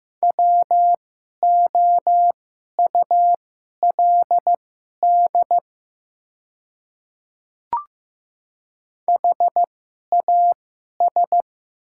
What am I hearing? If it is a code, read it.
Morse code, heard as a keyed tone, about 15 wpm, spaced as for 11 wpm, 700 Hz. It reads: WOULD HAS
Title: 2x Character Spacing - Top 100 Words - 15wpm